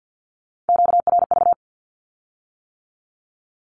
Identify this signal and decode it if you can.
Morse code, heard as a keyed tone, about 50 wpm, 700 Hz. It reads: QRV
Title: Q-Codes with Spoken Meaning - 50wpm